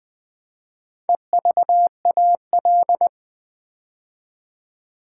Morse code, heard as a keyed tone, about 20 wpm, 700 Hz. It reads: EVAL
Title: Build A Word - 4 Letters At A Time - 10-Letter Words - 20wpm